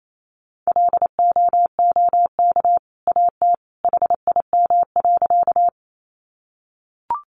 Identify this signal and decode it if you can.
Morse code, heard as a keyed tone, about 28 wpm, 700 Hz. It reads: LOOK AT HIM.
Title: Sentences from Top 100 Words 28wpm